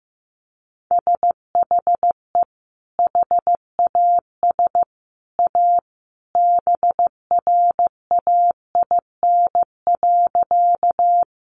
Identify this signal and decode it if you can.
Morse code, heard as a keyed tone, about 15 wpm, 700 Hz. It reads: SHE HAS A BRAIN.
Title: Sentences from Top 1100 Words 15wpm